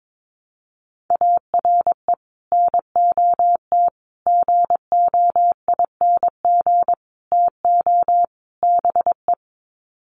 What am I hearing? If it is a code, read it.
Morse code, heard as a keyed tone, about 22 wpm, 700 Hz. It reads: ARE NOT GOING TO BE
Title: Sets of 5 Words Encore 22wpm